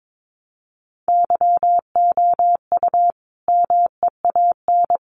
Morse code, heard as a keyed tone, about 22 wpm, 700 Hz. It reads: YOU MEAN
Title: Sentences from Top 600 Words 22wpm